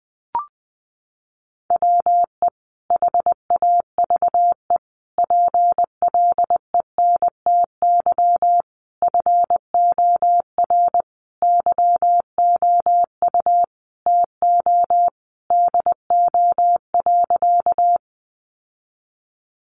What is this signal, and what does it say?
Morse code, heard as a keyed tone, about 20 wpm, 700 Hz. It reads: WE HAVE PLENTY FOR YOU TO DO.
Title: Sentences from Top 1300 Words 20wpm